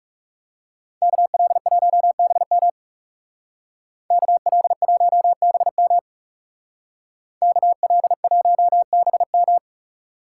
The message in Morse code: KL1BM KL1BM KL1BM

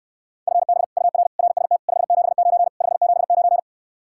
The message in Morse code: FB UR RST 579 579

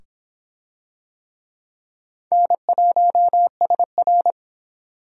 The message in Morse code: N1SR

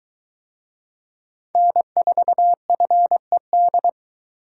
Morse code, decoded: N4FED

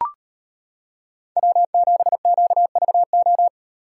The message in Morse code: W7QVO

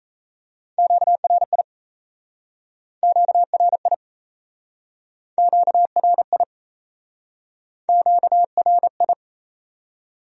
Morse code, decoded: QRI QRI QRI QRI

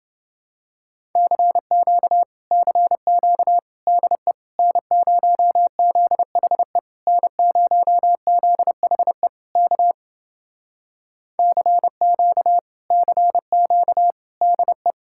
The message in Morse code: CQ CQ DE N0ZHE N0ZHE K CQ CQ DE

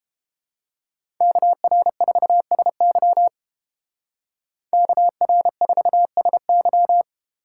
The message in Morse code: KR4SY KR4SY

